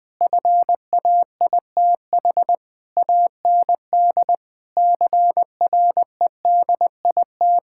Morse code, heard as a keyed tone, about 20 wpm, 700 Hz. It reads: FAITH AND CREDIT
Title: Sentences with 7-Word Phrases - 20wpm